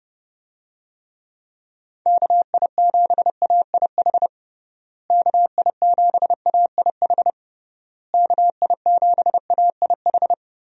Morse code, heard as a keyed tone, about 30 wpm, 700 Hz. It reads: KI7AIH KI7AIH KI7AIH
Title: USA Call Signs - 6 Characters - Repeated 3x - 30wpm